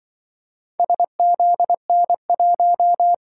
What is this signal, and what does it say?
Morse code, heard as a keyed tone, about 24 wpm, 700 Hz. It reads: SZN1
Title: Speed-Racing - Four Letter-Number Combinations - 20wpm